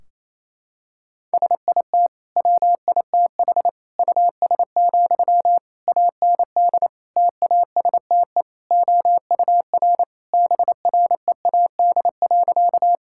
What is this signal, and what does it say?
Morse code, heard as a keyed tone, about 28 wpm, 700 Hz. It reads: SIT WITH US, AND TASTE OUR BREAD.
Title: Sentences from Top 1600 Words 28wpm